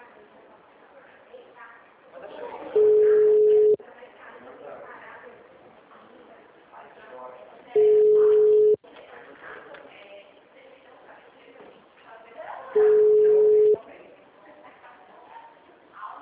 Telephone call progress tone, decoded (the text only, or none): ringback tone